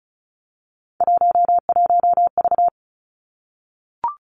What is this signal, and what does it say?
Morse code, heard as a keyed tone, about 36 wpm, 700 Hz. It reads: 11V
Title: Three Challenging Letter-Number Combinations 35wpm